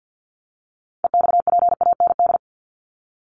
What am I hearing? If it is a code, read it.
Morse code, heard as a keyed tone, about 50 wpm, 700 Hz. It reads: EXPAND
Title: Top 1251-1500 Words 50wpm